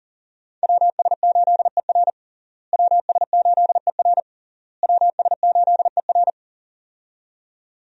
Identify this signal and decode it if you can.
Morse code, heard as a keyed tone, about 40 wpm, 700 Hz. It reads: WS8ER WS8ER WS8ER
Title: USA Call Signs - 5 Characters - Repeated 3x - 40wpm